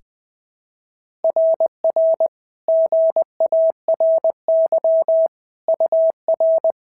Morse code, heard as a keyed tone, about 20 wpm, 650 Hz. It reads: RR GARY UR
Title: SOTA QSO Practice 20wpm